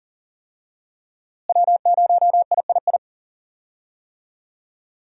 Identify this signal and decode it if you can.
Morse code, heard as a keyed tone, about 40 wpm, 700 Hz. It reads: W0III
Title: USA Call Signs - 5 Characters - Repeated 3x - 40wpm